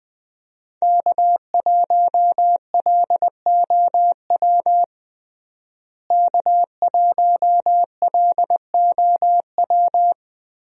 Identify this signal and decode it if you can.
Morse code, heard as a keyed tone, about 20 wpm, 700 Hz. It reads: K1LOW K1LOW
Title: USA Call Signs - 5 Characters - Repeated 3x - 20wpm